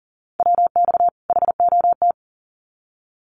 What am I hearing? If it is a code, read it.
Morse code, heard as a keyed tone, about 40 wpm, 700 Hz. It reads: WX HOT
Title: Common QSO Phrases 40wpm